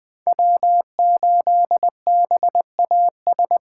Morse code, heard as a keyed tone, about 20 wpm, 700 Hz. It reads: W8BAS
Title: USA Call Signs - 2x Character Spacing - 20wpm